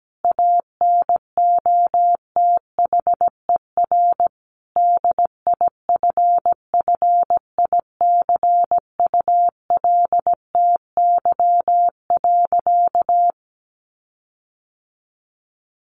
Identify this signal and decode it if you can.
Morse code, heard as a keyed tone, about 17 wpm, 700 Hz. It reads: ANOTHER DIFFICULTY.
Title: Sentences from Top 1500 Words 17wpm